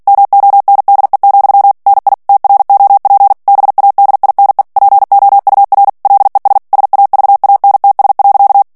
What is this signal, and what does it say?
Morse code, heard as a keyed tone, about 48 wpm, 800 Hz. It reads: MONDE, NI TROP BADINE POUR LES SAVANTS.
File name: CW / FICHIERS-MP3 / FRANCAIS TEXTE CLAIR 2 VITESSE 45 MOTS MINUTES